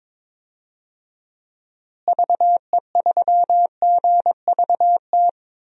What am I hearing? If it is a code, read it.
Morse code, heard as a keyed tone, about 22 wpm, 700 Hz. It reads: VE3GVT
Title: DXCC-100 Call Signs - 6 Characters - Repeated 3x - 22wpm